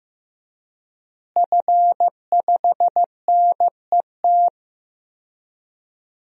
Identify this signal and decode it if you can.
Morse code, heard as a keyed tone, about 15 wpm, 700 Hz. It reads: F5NET
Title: DXCC-100 Call Signs - 5 Characters - Repeated 3x - 15wpm